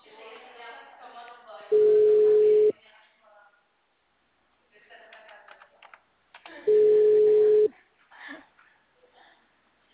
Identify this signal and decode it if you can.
Ringback tone (Europe, 425 Hz), with speech.
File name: out- -2039- - - .74529